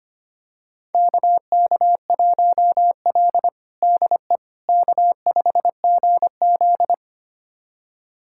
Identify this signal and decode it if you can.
Morse code, heard as a keyed tone, about 25 wpm, 700 Hz. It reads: KK1L DE K5GZ
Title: USA Callsign DE Callsign - 4 Characters - 25wpm